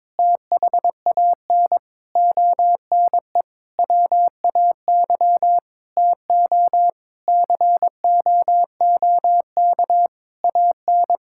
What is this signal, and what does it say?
Morse code, heard as a keyed tone, about 21 wpm, 700 Hz. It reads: THAN ONE WAY TO COOK AN
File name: Sentences with 7-Word Phrases - 22wpm